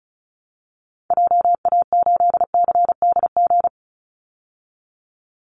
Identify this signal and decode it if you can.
Morse code, heard as a keyed tone, about 34 wpm, 700 Hz. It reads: JA8CDG
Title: Call Signs Repeated 3x - 35wpm